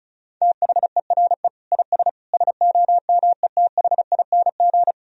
Morse code, heard as a keyed tone, about 34 wpm, 700 Hz. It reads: THERE IS SOMETHING
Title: Sentences from Top 1000 Words - Review Part 1 - 35wpm